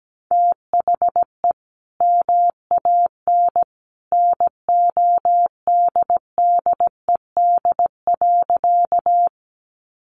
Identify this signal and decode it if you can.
Morse code, heard as a keyed tone, about 17 wpm, 700 Hz. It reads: THE MAN NODDED.
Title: A Sound Of Thunder 17wpm